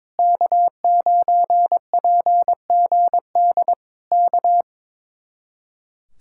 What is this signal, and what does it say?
Morse code, heard as a keyed tone, about 22 wpm, 700 Hz. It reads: K9PGD K
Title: Speed-Racing Common QSO Phrases - 22wpm